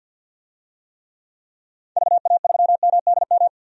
Morse code, heard as a keyed tone, about 46 wpm, 700 Hz. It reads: VA3MDM
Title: DXCC-100 Call Signs - 6 Characters - 50wpm